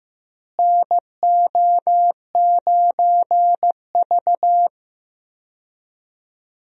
Morse code, heard as a keyed tone, about 15 wpm, 700 Hz. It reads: NO9V
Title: USA Call Signs - 4 Characters - Repeated 2x - 15wpm